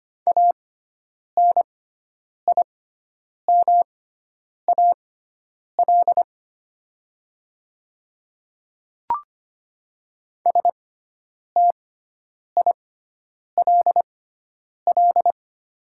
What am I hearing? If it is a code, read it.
Morse code, heard as a keyed tone, about 25 wpm, spaced as for 9 wpm, 700 Hz. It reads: ANIMAL STILL
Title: 6x Character Spacing - Top 300 Words - 25wpm